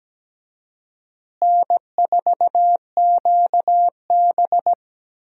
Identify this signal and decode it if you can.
Morse code, heard as a keyed tone, about 17 wpm, 700 Hz. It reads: N4QB
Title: USA Call Signs - 4 Characters - Repeated 2x - 17wpm